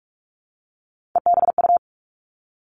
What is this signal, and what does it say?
Morse code, heard as a keyed tone, about 46 wpm, 700 Hz. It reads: EBU
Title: Three Challenging Letter-Number Combinations 45wpm